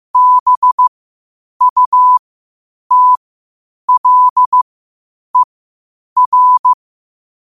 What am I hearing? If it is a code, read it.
Morse code, heard as a keyed tone, about 15 wpm, spaced as for 8 wpm, 1000 Hz. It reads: BUTLER
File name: qotdfiles / 8wpm